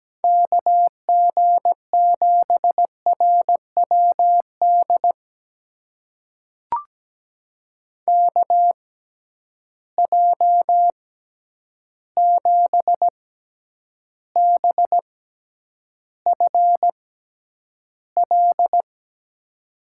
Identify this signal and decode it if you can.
Morse code, heard as a keyed tone, about 17 wpm, spaced as for 9 wpm, 700 Hz. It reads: KG7RWD KJ7BFL
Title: USA Call Signs - 6 Characters - 6x Character Spacing - 17wpm